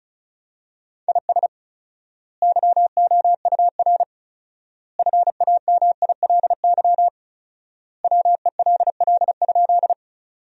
Morse code, heard as a keyed tone, about 36 wpm, 700 Hz. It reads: IS YOUR FAMILY WELL?
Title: 4x Word-Spacing - Sentences from Top 400 Words - 35wpm